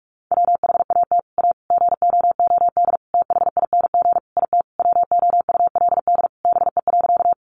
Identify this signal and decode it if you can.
Morse code, heard as a keyed tone, about 44 wpm, 700 Hz. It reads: WHAT A GOOD THING IT WOULD BE.
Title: Sentences from Top 200 Words 45wpm